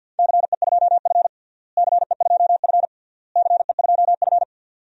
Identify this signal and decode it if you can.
Morse code, heard as a keyed tone, about 50 wpm, 700 Hz. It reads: CE2F CE2F CE2F
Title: DXCC-100 Call Signs - 4 Characters - Repeated 3x - 50wpm